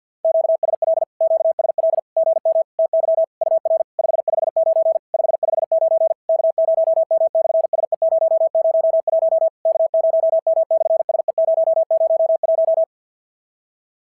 Morse code, heard as a keyed tone, about 54 wpm, 650 Hz. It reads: QSL QSL GM TY UR 559 559 K0M/SE001 K0M/SE001
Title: SOTA QSO Practice 50wpm